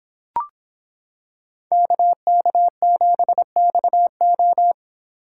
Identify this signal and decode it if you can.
Morse code, heard as a keyed tone, about 26 wpm, 700 Hz. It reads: KK7XO